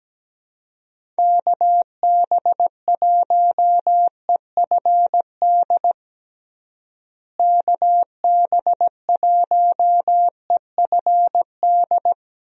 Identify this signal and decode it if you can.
Morse code, heard as a keyed tone, about 17 wpm, 700 Hz. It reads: KB1EFD KB1EFD
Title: USA Call Signs - 6 Characters - Repeated 2x - 17wpm